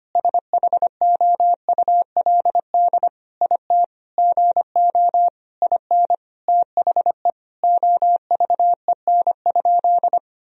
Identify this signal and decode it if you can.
Morse code, heard as a keyed tone, about 25 wpm, 700 Hz. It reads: SHOULD IT GO IN THE OVEN?
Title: Sentences from Top 1000 Words - Review Part 4 - 25wpm